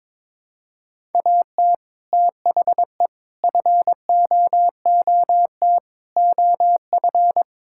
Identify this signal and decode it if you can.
Morse code, heard as a keyed tone, about 21 wpm, 700 Hz. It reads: AT THE FOOT OF
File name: practice / 22wpm / Build A Phrase - 5-Word Phrases - 22wpm